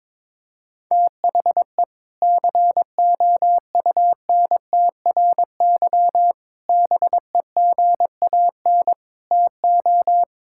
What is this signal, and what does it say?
Morse code, heard as a keyed tone, about 21 wpm, 700 Hz. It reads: THE COUNTRY BEGAN TO
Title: Sentences from Top 1000 Words - Review Part 5 - 22wpm